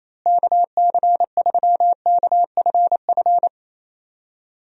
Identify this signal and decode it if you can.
Morse code, heard as a keyed tone, about 28 wpm, 700 Hz. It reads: KC3KFF
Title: USA Call Signs - 6 Characters - Repeated 2x - 28wpm